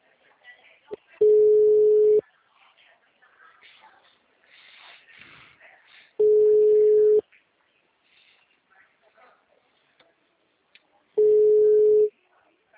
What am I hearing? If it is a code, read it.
Ringback tone (Europe, 425 Hz), with speech.